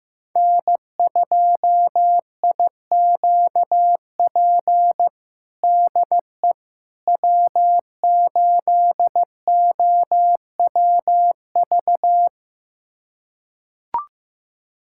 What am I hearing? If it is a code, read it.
Morse code, heard as a keyed tone, about 15 wpm, 700 Hz. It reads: N2IQP DE W8OWV
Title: USA Callsign DE Callsign - 5 Characters - 15wpm